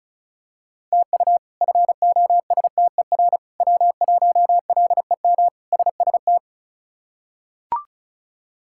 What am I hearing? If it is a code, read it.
Morse code, heard as a keyed tone, about 36 wpm, 700 Hz. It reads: TU FOSTER W1LEM SST
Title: K1USN SST QSO Practice - 35wpm